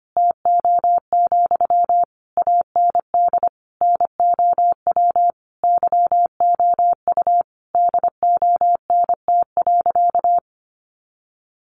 Morse code, heard as a keyed tone, about 25 wpm, 700 Hz. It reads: TO, AND NOW YOU DONT.
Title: Sentences from Top 600 Words 25wpm